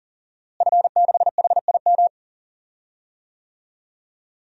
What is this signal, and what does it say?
Morse code, heard as a keyed tone, about 40 wpm, 700 Hz. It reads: F6HIM
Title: DXCC-100 Call Signs - 5 Characters - Repeated 2x - 40wpm